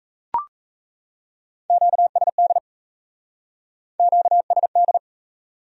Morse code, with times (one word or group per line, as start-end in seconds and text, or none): 1.7-2.6 QSD
4.0-5.0 QSD